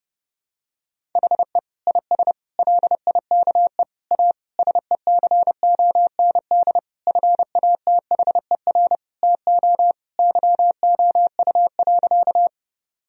Morse code, heard as a keyed tone, about 30 wpm, 700 Hz. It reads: HE IS LIKE A SECOND FATHER TO YOU.
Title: Sentences from Top 1000 Words - Review Part 1 - 30wpm